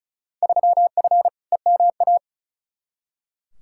Morse code, heard as a keyed tone, about 36 wpm, 700 Hz. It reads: 3F EMA
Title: Field Day Exchanges Encore - 35wpm